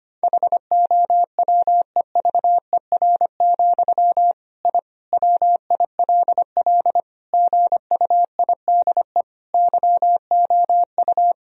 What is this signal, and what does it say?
Morse code, heard as a keyed tone, about 25 wpm, 700 Hz. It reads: HOWEVER, I WILL GUIDE YOU
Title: Sentences from Top 1000 Words - Review Part 3 - 25wpm